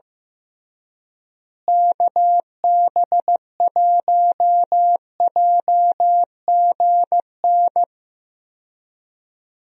Morse code, heard as a keyed tone, about 15 wpm, 700 Hz. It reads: KB1JGN
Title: USA Call Signs - 6 Characters - Repeated 3x - 15wpm